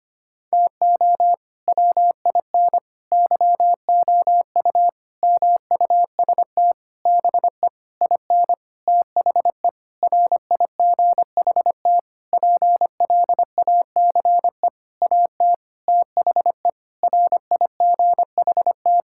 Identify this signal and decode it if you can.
Morse code, heard as a keyed tone, about 25 wpm, 700 Hz. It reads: TO WIN YOU MUST BE IN THE RIGHT PLACE AT THE RIGHT
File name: practice / 25wpm / Sentences with 9-Word Phrases - 25wpm